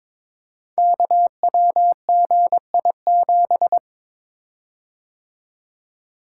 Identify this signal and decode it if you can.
Morse code, heard as a keyed tone, about 22 wpm, 700 Hz. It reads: KWGI7
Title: Code Groups - 5 Letter:Numbers - 22wpm